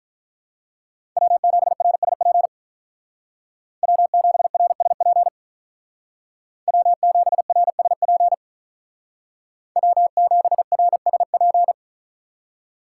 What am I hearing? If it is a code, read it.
Morse code, heard as a keyed tone, about 47 wpm, 700 Hz. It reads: W7RSP W7RSP W7RSP W7RSP